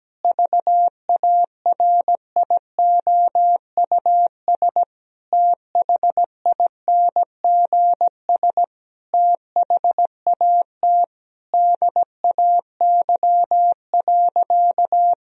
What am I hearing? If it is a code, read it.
Morse code, heard as a keyed tone, about 17 wpm, 700 Hz. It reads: VARIOUS THINGS THAT DAY.
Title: Sentences from Top 1000 Words - Review Part 4 - 17wpm